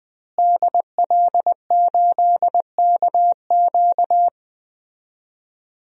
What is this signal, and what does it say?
Morse code, heard as a keyed tone, about 20 wpm, 700 Hz. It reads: DL8KQ
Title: DXCC-100 Call Signs - 5 Characters - Repeated 3x - 20wpm